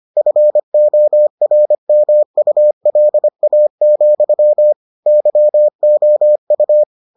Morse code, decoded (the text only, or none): FORMULA, YOU